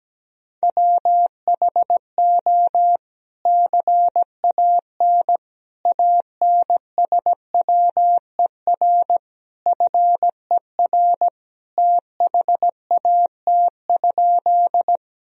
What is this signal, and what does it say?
Morse code, heard as a keyed tone, about 17 wpm, 700 Hz. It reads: WHO CAN ANSWER FER THAT?
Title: Sentences from Top 300 Words and Abbreviations 17wpm